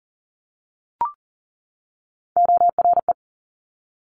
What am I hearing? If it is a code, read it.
Morse code, heard as a keyed tone, about 38 wpm, 700 Hz. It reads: ORE
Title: Top Three Letter Combinations in Words 40wpm